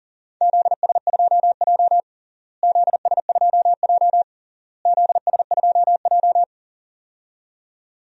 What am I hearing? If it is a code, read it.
Morse code, heard as a keyed tone, about 40 wpm, 700 Hz. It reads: ZS2J ZS2J ZS2J
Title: DXCC-100 Call Signs - 4 Characters - Repeated 3x - 40wpm